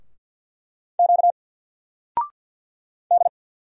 Morse code, heard as a keyed tone, about 50 wpm, 700 Hz. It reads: Q D
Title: Single Letter-Number 50wpm